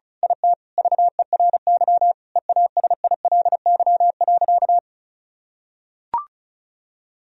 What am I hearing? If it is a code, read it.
Morse code, heard as a keyed tone, about 36 wpm, 700 Hz. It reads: IT VERY EASILY.
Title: Sentences from Top 1000 Words - Review Part 1 - 35wpm